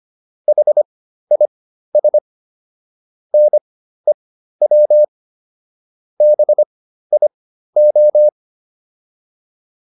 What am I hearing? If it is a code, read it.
Morse code, heard as a keyed tone, about 25 wpm, spaced as for 13 wpm, 600 Hz. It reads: HIS NEW BIO